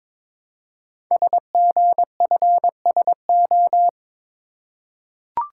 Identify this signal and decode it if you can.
Morse code, heard as a keyed tone, about 23 wpm, 700 Hz. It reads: SGFSO
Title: Code Groups - 5 Letters - 22wpm